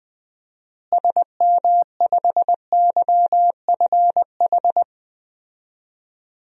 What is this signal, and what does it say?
Morse code, heard as a keyed tone, about 20 wpm, 700 Hz. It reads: SM5YFH
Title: DXCC-100 Call Signs - 6 Characters - Repeated 2x - 20wpm